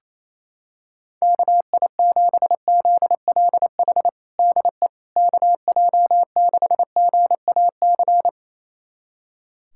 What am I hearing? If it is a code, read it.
Morse code, heard as a keyed tone, about 27 wpm, 700 Hz. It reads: KI7ZLH DE KJ6GAC
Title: USA Callsign DE Callsign - 6 Characters - 28wpm